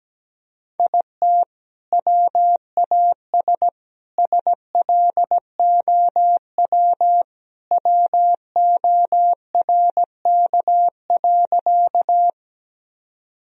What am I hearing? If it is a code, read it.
Morse code, heard as a keyed tone, about 17 wpm, 700 Hz. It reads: IT WAS SLOW WORK.
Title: Sentences from Top 1000 Words - Review Part 2 - 17wpm